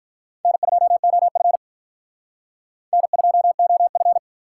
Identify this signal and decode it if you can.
Morse code, heard as a keyed tone, about 45 wpm, 700 Hz. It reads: N2OF N2OF